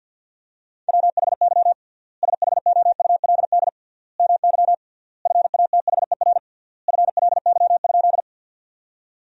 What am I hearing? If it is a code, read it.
Morse code, heard as a keyed tone, about 52 wpm, 700 Hz. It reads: WHY SHOULD MY FATHER FLY?
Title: 3x Word-Spacing - Sentences from Top 500 Words - 50wpm